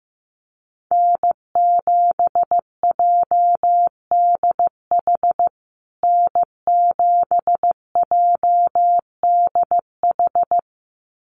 Morse code, heard as a keyed tone, about 15 wpm, 700 Hz. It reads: N7JDH N7JDH
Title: USA Call Signs Repeated 2x 15wpm